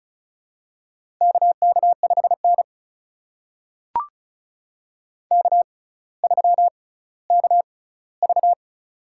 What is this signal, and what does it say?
Morse code, heard as a keyed tone, about 34 wpm, spaced as for 18 wpm, 700 Hz. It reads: KK5N K3KV
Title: USA Call Signs - 4 Characters - 6x Character Spacing - 35wpm